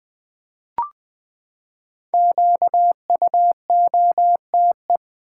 Morse code, heard as a keyed tone, about 21 wpm, 700 Hz. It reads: QUOTE
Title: Five Letter Words - 20wpm